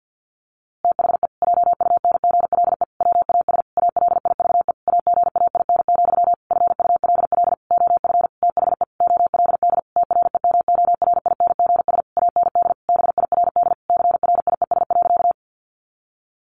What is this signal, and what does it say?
Morse code, heard as a keyed tone, about 50 wpm, 700 Hz. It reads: THE JUNGLE WAS ALIVE AGAIN, FULL OF THE OLD TREMORINGS AND BIRD CRIES.
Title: A Sound Of Thunder 50wpm